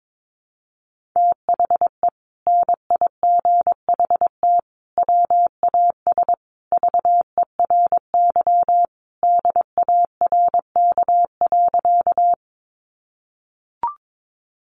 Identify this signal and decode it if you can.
Morse code, heard as a keyed tone, about 22 wpm, 700 Hz. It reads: THE NIGHT WAS VERY DARK.
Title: Sentences from Top 500 Words 22wpm